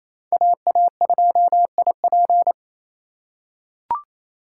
Morse code, heard as a keyed tone, about 28 wpm, 700 Hz. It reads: AA2IP
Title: USA Call Signs - 5 Characters - 4x Character Spacing - 28wpm